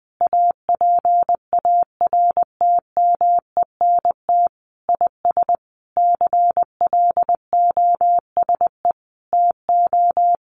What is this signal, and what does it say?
Morse code, heard as a keyed tone, about 20 wpm, 700 Hz. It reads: APARTMENT IS CLOSE TO
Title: Sentences from Top 900 Words 20wpm